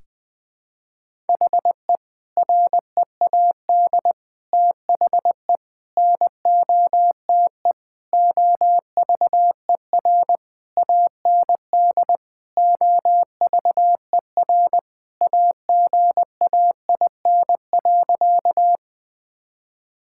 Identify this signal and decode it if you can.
Morse code, heard as a keyed tone, about 20 wpm, 700 Hz. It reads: HE READ THE NOTE OVER AND OVER AGAIN.
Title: Sentences with 4-Word Phrases - Spoken Beginning - 20wpm